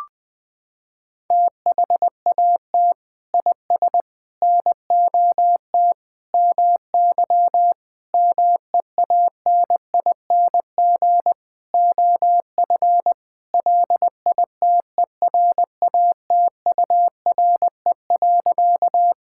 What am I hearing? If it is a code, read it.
Morse code, heard as a keyed tone, about 20 wpm, 700 Hz. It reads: THAT IS NOT MY MEANING OF LITERATURE.
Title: Sentences from Top 1000 Words - Review Part 6 - 20wpm